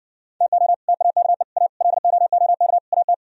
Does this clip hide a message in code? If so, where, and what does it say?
Yes: Morse code from 0.4 to 3.1 s, heard as a keyed tone, about 69 wpm, 700 Hz. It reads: TO TAKE A LOOK AT